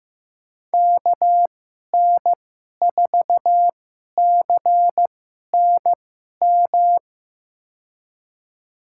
Morse code, heard as a keyed tone, about 15 wpm, spaced as for 11 wpm, 700 Hz. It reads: KN4CNM
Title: USA Call Signs - 2x Character Spacing - 15wpm